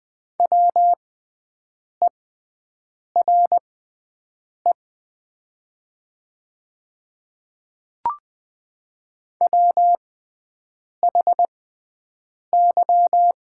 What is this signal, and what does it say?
Morse code, heard as a keyed tone, about 20 wpm, spaced as for 7 wpm, 700 Hz. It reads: WERE WHY
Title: 6x Character Spacing - Top 200 Words - 20wpm